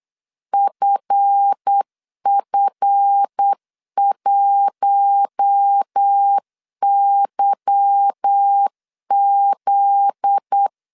Morse code, heard as a keyed tone, about 8 wpm, 800 Hz. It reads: FF1YZ
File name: LEZIONI / 10 SIGNS / 10C DOT 8wpm 3649